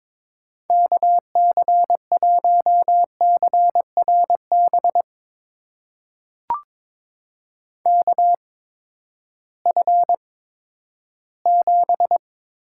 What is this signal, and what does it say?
Morse code, heard as a keyed tone, about 23 wpm, spaced as for 13 wpm, 700 Hz. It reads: KC1CRB KF7
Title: USA Call Signs - 6 Characters - 8x Character Spacing - 22wpm